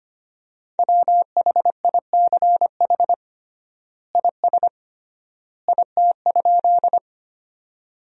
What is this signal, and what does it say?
Morse code, heard as a keyed tone, about 25 wpm, 700 Hz. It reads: WHICH IS IT?
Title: 3x Word-Spacing - Sentences from Top 100 Words - 25wpm